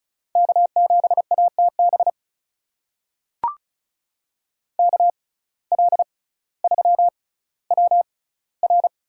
Morse code, heard as a keyed tone, about 36 wpm, spaced as for 18 wpm, 700 Hz. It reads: K7ATB KL3WR
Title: USA Call Signs - 5 Characters - 6x Character Spacing - 35wpm